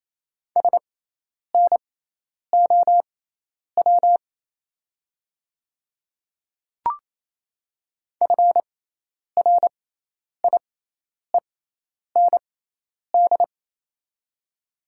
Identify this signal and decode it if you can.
Morse code, heard as a keyed tone, about 28 wpm, spaced as for 10 wpm, 700 Hz. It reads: SNOW FRIEND
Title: 6x Character Spacing - Top 500 Words - 28wpm